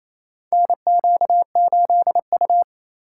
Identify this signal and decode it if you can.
Morse code, heard as a keyed tone, about 27 wpm, 700 Hz. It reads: NQ8U
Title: USA Call Signs - 4 Characters - 4x Character Spacing - 28wpm